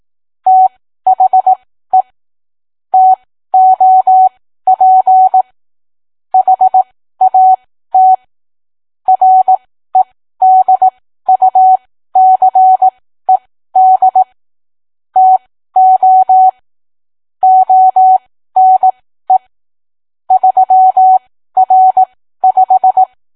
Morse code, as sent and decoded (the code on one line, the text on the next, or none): - .... / - --- .--. / .... .- - / .-. -.. ..- -.-. -.. / - --- / --- -. / ...-- .-. .....
THE TOP HAT REDUCED TO ONE 3R5